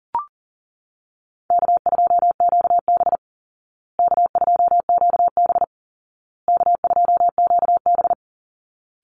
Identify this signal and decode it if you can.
Morse code, heard as a keyed tone, about 40 wpm, 700 Hz. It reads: K2QB K2QB K2QB
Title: Call Signs Repeated 2x - 40wpm